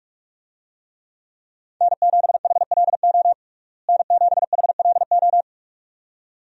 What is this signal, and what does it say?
Morse code, heard as a keyed tone, about 41 wpm, 700 Hz. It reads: N7HLO N7HLO
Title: USA Call Signs - 5 Characters - Repeated 2x - 45wpm